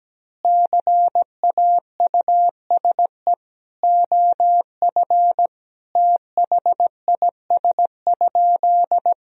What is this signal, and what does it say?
Morse code, heard as a keyed tone, about 17 wpm, 700 Hz. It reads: CAUSE OF THIS?
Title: Sentences from Top 1000 Words - Review Part 1 - 17wpm